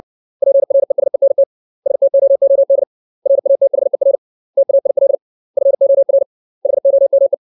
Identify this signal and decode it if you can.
Morse code, heard as a keyed tone, about 61 wpm, 550 Hz. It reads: PRESENT STOOD RATHER TAIL FOR HOME
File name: 632 words / 632 Words 60WPM